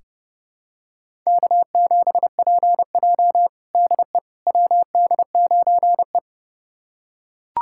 Morse code, heard as a keyed tone, about 32 wpm, 700 Hz. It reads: K7PJ DE WD9E
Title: USA Callsign DE Callsign - 4 Characters - 30wpm